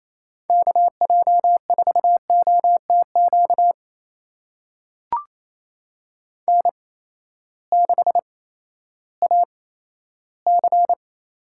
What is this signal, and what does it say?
Morse code, heard as a keyed tone, about 28 wpm, spaced as for 14 wpm, 700 Hz. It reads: KJ4OTQ N6AC
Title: USA Call Signs - 8x Character Spacing - 28wpm